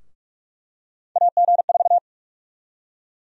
Morse code, heard as a keyed tone, about 42 wpm, 700 Hz. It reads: AG4
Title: USA Call Sign Prefixes - Repeated 2x - 45wpm